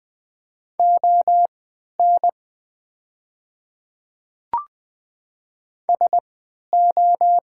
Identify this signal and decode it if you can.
Morse code, heard as a keyed tone, about 20 wpm, spaced as for 11 wpm, 700 Hz. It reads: ON SO
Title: 3x Character Spacing - Top 300 Words - 20wpm